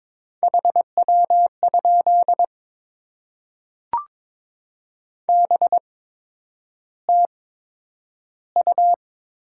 Morse code, heard as a keyed tone, about 22 wpm, spaced as for 9 wpm, 700 Hz. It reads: HW? BTU
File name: Top 20 QSO Elements - 8x Character Spacing - 22wpm